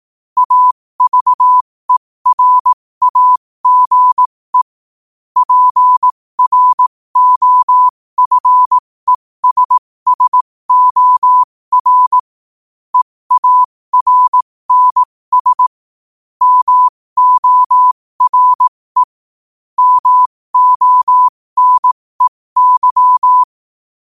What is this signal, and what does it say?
Morse code, heard as a keyed tone, about 18 wpm, spaced as for 15 wpm, 1000 Hz. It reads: AVERAGE PROFESSOR EARNS MORE MONEY
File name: qotdfiles / 15wpm